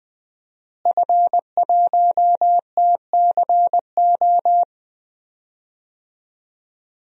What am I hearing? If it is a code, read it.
Morse code, heard as a keyed tone, about 20 wpm, 700 Hz. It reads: F1TCO
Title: DXCC-100 Call Signs - 5 Characters - Repeated 3x - 20wpm